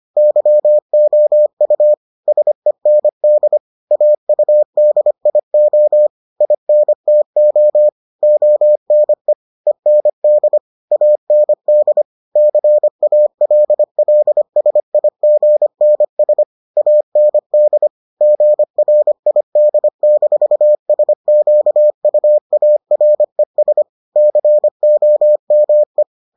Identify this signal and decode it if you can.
Morse code, heard as a keyed tone, about 25 wpm, 600 Hz. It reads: YOU SEND AUDIO INTO ONE END AND CALLSIGNS AND GRID-SQUARES COME